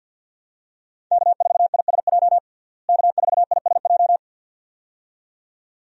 Morse code, heard as a keyed tone, about 50 wpm, 700 Hz. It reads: K4ISJ K4ISJ
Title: USA Call Signs - 5 Characters - Repeated 2x - 50wpm